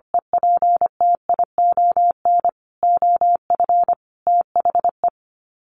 Morse code, heard as a keyed tone, about 25 wpm, 690 Hz. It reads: EPTION OF THE
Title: Sets of 5 Words Encore 25wpm